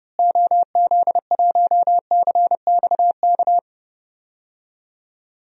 Morse code, heard as a keyed tone, about 30 wpm, 700 Hz. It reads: OZ1CXK
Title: DXCC-100 Call Signs - 6 Characters - Repeated 3x - 30wpm